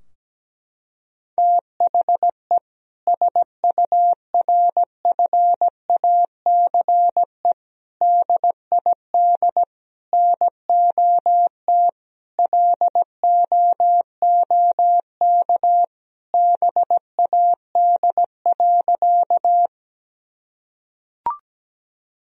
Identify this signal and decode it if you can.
Morse code, heard as a keyed tone, about 17 wpm, 700 Hz. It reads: THE SURFACE DID NOT LOOK BAD.
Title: Sentences from Top 1000 Words - Review Part 3 - 17wpm